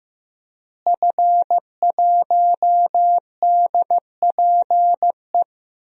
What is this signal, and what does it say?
Morse code, heard as a keyed tone, about 15 wpm, 700 Hz. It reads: F1DPE
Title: DXCC-100 Call Signs - 5 Characters - Repeated 3x - 15wpm